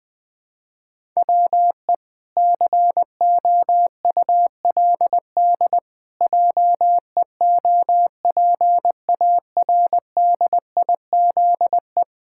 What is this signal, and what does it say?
Morse code, heard as a keyed tone, about 21 wpm, 700 Hz. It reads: WE COULD JEOPARDIZE
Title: Pangram Challenge v2 - 20wpm